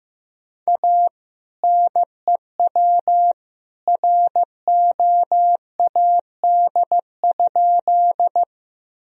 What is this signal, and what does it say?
Morse code, heard as a keyed tone, about 15 wpm, 700 Hz. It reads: A NEW ROAD?
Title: Sentences from Top 1000 Words - Review Part 2 - 15wpm